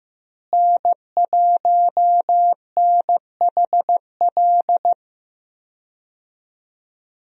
Morse code, heard as a keyed tone, about 15 wpm, 700 Hz. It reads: N1NHL
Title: USA Call Signs - 5 Characters - Repeated 2x - 15wpm